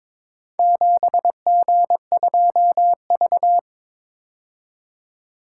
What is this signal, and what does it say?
Morse code, heard as a keyed tone, about 22 wpm, 690 Hz. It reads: 7G2V
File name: practice / 22wpm / Four Letter-Number Combinations - 22wpm